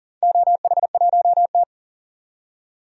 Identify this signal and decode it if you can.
Morse code, heard as a keyed tone, about 37 wpm, 700 Hz. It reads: OH1T
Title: DXCC-100 Call Signs - 4 Characters - 40wpm